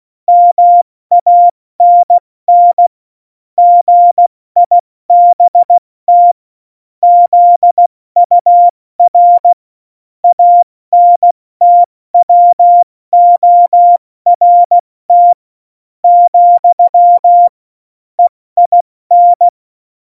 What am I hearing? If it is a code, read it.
Morse code, heard as a keyed tone, about 16 wpm, spaced as for 14 wpm, 700 Hz. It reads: MANN GIBT ZUR ANTWORT, EIN